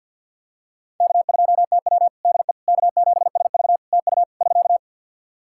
Morse code, heard as a keyed tone, about 51 wpm, 700 Hz. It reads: K2TW DE K7SV TU <SK>
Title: K1USN SST QSO Practice 50wpm